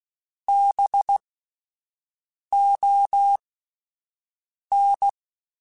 Morse code, heard as a keyed tone, about 16 wpm, spaced as for 5 wpm, 800 Hz. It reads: BON